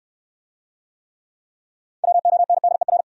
Morse code, heard as a keyed tone, about 67 wpm, 700 Hz. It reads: WONDER